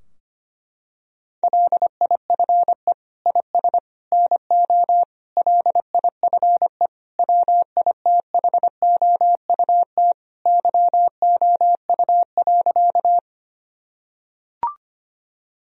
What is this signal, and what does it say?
Morse code, heard as a keyed tone, about 25 wpm, 700 Hz. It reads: LIFE IS NO LIFE WITHOUT YOU.
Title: Sentences from Top 1000 Words - Review Part 3 - 25wpm